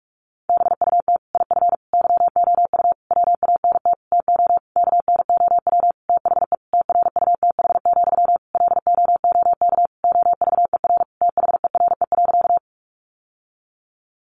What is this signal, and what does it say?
Morse code, heard as a keyed tone, about 44 wpm, 700 Hz. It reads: BUT IF YOU WANT TO KNOW THE TRUTH, LOOK OVER THERE.
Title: Sentences from Top 800 Words 45wpm